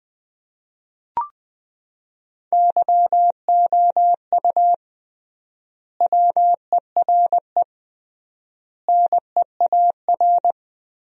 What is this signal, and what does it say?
Morse code, heard as a keyed tone, about 20 wpm, 700 Hz. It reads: YOU WERE NEAR
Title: 3x Word-Spacing - Sentences from Top 200 Words - 20wpm